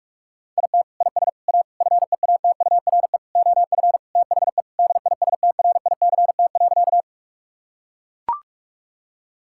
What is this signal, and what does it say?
Morse code, heard as a keyed tone, about 41 wpm, 700 Hz. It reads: IT IS A FEATURE OF THE DISTRICT.
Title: Sentences from Top 1600 Words 45wpm